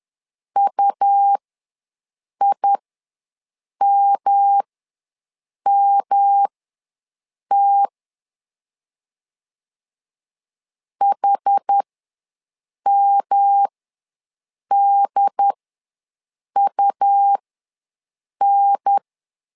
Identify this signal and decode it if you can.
Morse code, heard as a keyed tone, about 10 wpm, spaced as for 6 wpm, 800 Hz. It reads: UIMMT HMDUN
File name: LEZIONI / 3 MWRO / 3 M ws 10wpm 2121